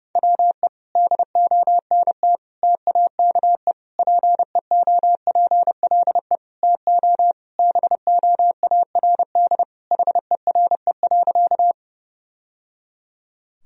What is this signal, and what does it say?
Morse code, heard as a keyed tone, about 30 wpm, 700 Hz. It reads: WE DONT TAKE PEOPLE TO BOARD HERE.
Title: Sentences from Top 1000 Words - Review Part 4 - 30wpm